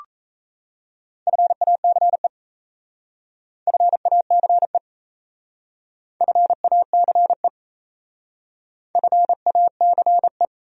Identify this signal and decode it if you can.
Morse code, heard as a keyed tone, about 33 wpm, 700 Hz. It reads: FACE FACE FACE FACE
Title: Speed-Racing Top 500 Words 28wpm